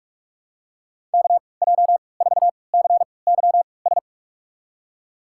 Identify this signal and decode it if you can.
Morse code, heard as a keyed tone, about 43 wpm, spaced as for 25 wpm, 700 Hz. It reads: KJ4CYS
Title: USA Call Signs - 3x Character Spacing - 45wpm